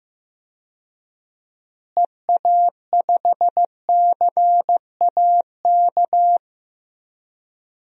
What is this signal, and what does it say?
Morse code, heard as a keyed tone, about 15 wpm, 700 Hz. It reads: EA5CAK
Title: DXCC-100 Call Signs - 6 Characters - Repeated 3x - 15wpm